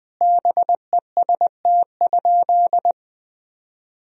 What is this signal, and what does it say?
Morse code, heard as a keyed tone, about 20 wpm, 700 Hz. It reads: BEST?
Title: Sentences from Top 1000 Words - Review Part 1 - 20wpm